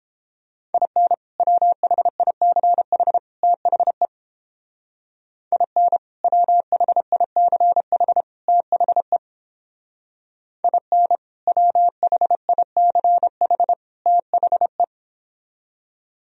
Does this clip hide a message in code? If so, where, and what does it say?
Yes: Morse code from 0.7 to 14.8 s, heard as a keyed tone, about 29 wpm, 700 Hz. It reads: IN WHICH THE IN WHICH THE IN WHICH THE